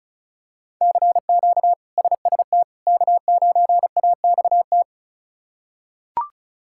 Morse code, heard as a keyed tone, about 36 wpm, 700 Hz. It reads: CQ SST K9AXT
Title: K1USN SST QSO Practice - 35wpm